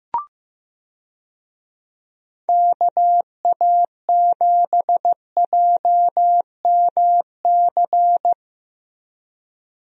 Morse code, heard as a keyed tone, about 14 wpm, 700 Hz. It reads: KA7JMC
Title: USA Call Signs - 6 Characters - Repeated 2x - 15wpm